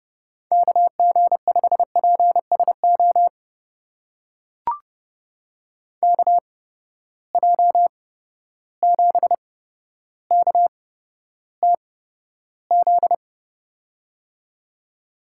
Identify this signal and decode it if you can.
Morse code, heard as a keyed tone, about 30 wpm, spaced as for 13 wpm, 700 Hz. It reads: KG5PSO KJ7KTZ
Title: USA Call Signs - 8x Character Spacing - 30wpm